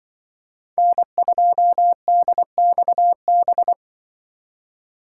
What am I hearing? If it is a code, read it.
Morse code, heard as a keyed tone, about 24 wpm, 700 Hz. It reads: N2DXB
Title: Speed Racing - USA 1x3 Call Signs 20wpm